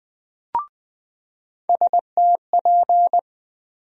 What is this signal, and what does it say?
Morse code, heard as a keyed tone, about 21 wpm, 700 Hz. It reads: STP
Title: USA Call Sign Postfixes - 3 Letters 20wpm